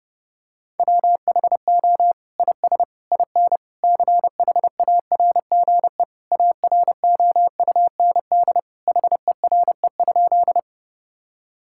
Morse code, heard as a keyed tone, about 30 wpm, 700 Hz. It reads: WHO IS IN CHARGE AROUND HERE?
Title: Sentences from Top 1000 Words - Review Part 6 - 30wpm